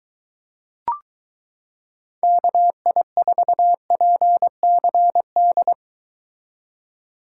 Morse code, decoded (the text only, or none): KI4PCD